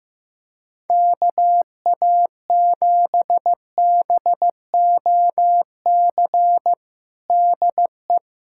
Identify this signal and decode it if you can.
Morse code, heard as a keyed tone, about 15 wpm, 700 Hz. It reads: KA7BOC DE
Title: USA Callsign DE Callsign - 6 Characters - 15wpm